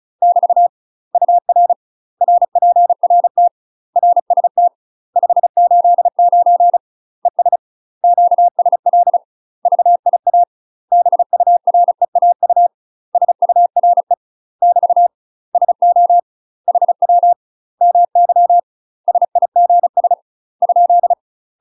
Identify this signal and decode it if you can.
Morse code, heard as a keyed tone, about 36 wpm, 700 Hz. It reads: <BT> UR RPRT RST 589 ES QSL VIA BUREAU SURE <BT> SO HW MY SIGS ?
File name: UK MASTER QSOs TRAINING / 35WPM / UK MASTER QSOs 0017